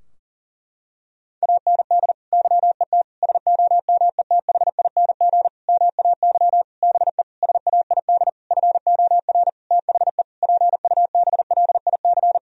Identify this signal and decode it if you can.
Morse code, heard as a keyed tone, about 37 wpm, 700 Hz. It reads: AND YET SOMETHING MAY BE SAID FOR THE PUBLIC
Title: Sentences from Top 1000 Words - Review Part 3 - 40wpm